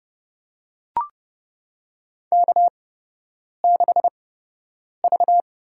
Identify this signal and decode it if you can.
Morse code, heard as a keyed tone, about 31 wpm, spaced as for 8 wpm, 700 Hz. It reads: K6V
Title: USA Call Signs - 8x Character Spacing - 30wpm